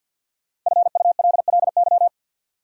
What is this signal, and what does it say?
Morse code, heard as a keyed tone, about 53 wpm, 700 Hz. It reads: FULLY
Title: Sentences from Top 1300 Words - New Word Drill Alt - 50wpm